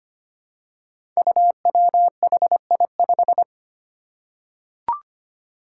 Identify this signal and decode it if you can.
Morse code, heard as a keyed tone, about 25 wpm, 700 Hz. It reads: UWHI5
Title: Five Challenging Letter-Number Combinations 25wpm